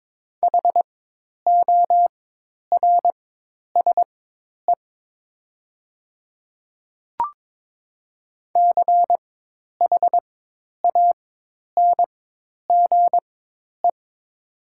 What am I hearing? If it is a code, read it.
Morse code, heard as a keyed tone, about 21 wpm, spaced as for 10 wpm, 700 Hz. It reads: HORSE CHANGE
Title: 4x Character Spacing - Top 500 Words - 22wpm